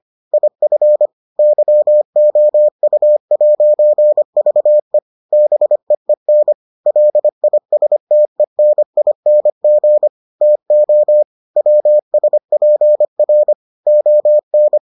Morse code, heard as a keyed tone, about 25 wpm, 600 Hz. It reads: IF YOU'VE BEEN LISTENING TO WSPR ON